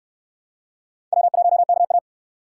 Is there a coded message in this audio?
Yes: Morse code from 1.1 to 2.0 s, heard as a keyed tone, about 65 wpm, 700 Hz. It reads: W1KA